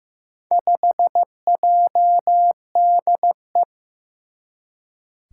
Morse code, heard as a keyed tone, about 15 wpm, 700 Hz. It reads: 5JDE
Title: Four Challenging Letter-Number Combinations 15wpm